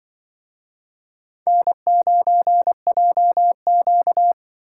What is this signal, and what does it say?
Morse code, heard as a keyed tone, about 25 wpm, 700 Hz. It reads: N9JQ
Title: Speed Racing - USA 1x2 Call Signs 20wpm